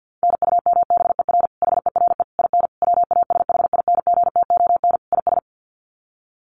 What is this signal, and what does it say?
Morse code, heard as a keyed tone, about 48 wpm, 700 Hz. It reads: NUMBER HERE IN WASHINGTON IS
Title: Sets of 5 Words Encore2 50wpm